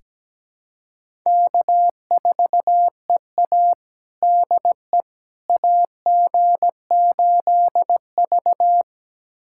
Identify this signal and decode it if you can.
Morse code, heard as a keyed tone, about 17 wpm, 700 Hz. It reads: K4EA DE AG8V
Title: USA Callsign DE Callsign - 4 Characters - 17wpm